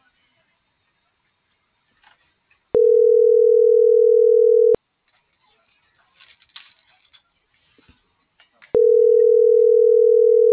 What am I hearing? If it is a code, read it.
Ringback tone (North America), 440 + 480 Hz.
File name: CENTRAL TERRESTRE